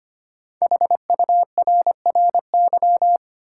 Morse code, heard as a keyed tone, about 25 wpm, 700 Hz. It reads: HURRY
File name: 5 Letter Words - First 3 Letters Given - 25wpm